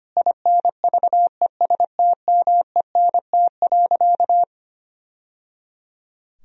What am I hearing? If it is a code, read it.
Morse code, heard as a keyed tone, about 25 wpm, 700 Hz. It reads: INVESTMENT.
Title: Sentences from Top 1000 Words - Review Part 4 - 25wpm